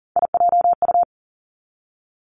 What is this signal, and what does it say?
Morse code, heard as a keyed tone, about 41 wpm, 700 Hz. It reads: IJU
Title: Three Challenging Letter-Number Combinations 40wpm